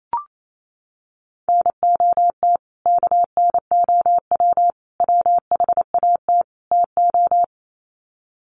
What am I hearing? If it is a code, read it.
Morse code, heard as a keyed tone, about 28 wpm, 700 Hz. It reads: NOT KNOW WHAT TO
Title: Sets of 4 Words 28wpm